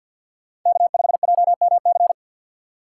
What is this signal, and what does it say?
Morse code, heard as a keyed tone, about 53 wpm, 690 Hz. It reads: K5JMC
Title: USA Call Signs - 5 Characters - 4x Character Spacing - 50wpm